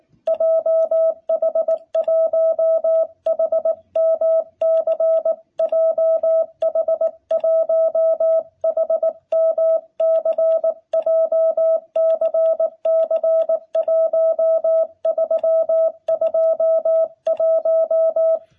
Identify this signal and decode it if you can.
Morse code, heard as a keyed tone, about 19 wpm, 660 Hz. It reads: JH1HMC JH1HMC JCC1321
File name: jh8cbh / 2021cw